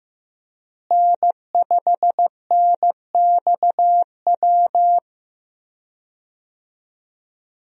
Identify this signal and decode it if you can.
Morse code, heard as a keyed tone, about 14 wpm, 700 Hz. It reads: N5NXW
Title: USA Call Signs - 5 Characters - Repeated 2x - 15wpm